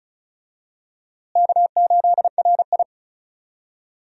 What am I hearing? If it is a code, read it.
Morse code, heard as a keyed tone, about 36 wpm, 700 Hz. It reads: K8RI
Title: USA Call Signs - 4 Characters - Repeated 3x - 35wpm